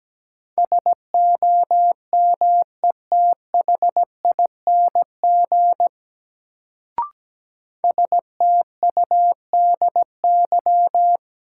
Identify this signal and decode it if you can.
Morse code, heard as a keyed tone, about 17 wpm, 700 Hz. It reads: SOMETHING STUDY
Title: Sets of 5 Words - New Word Drill - 17wpm